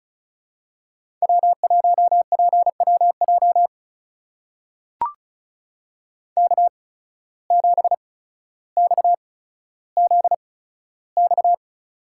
Morse code, heard as a keyed tone, about 36 wpm, spaced as for 15 wpm, 700 Hz. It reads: W1PWJ K7XZX
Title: USA Call Signs - 5 Characters - 8x Character Spacing - 35wpm